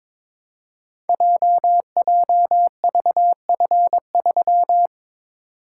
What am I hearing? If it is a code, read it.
Morse code, heard as a keyed tone, about 22 wpm, 700 Hz. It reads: JJVF3
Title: Code Groups - 5 Letter:Numbers - 22wpm